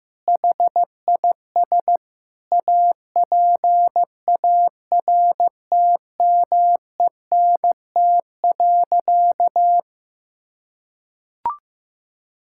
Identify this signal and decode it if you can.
Morse code, heard as a keyed tone, about 15 wpm, 700 Hz. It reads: HIS APARTMENT.
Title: Sentences from Top 1000 Words - Review Part 6 - 15wpm